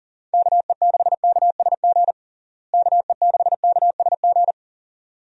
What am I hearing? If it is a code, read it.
Morse code, heard as a keyed tone, about 40 wpm, 700 Hz. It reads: KE6KSG KE6KSG
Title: USA Call Signs - 6 Characters - Repeated 2x - 40wpm